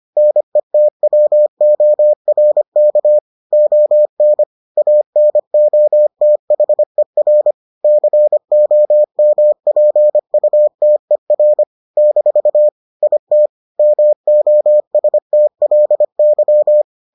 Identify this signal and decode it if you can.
Morse code, heard as a keyed tone, about 25 wpm, 600 Hz. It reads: NETWORK ON ANOTHER COMPUTER - IT MOSTLY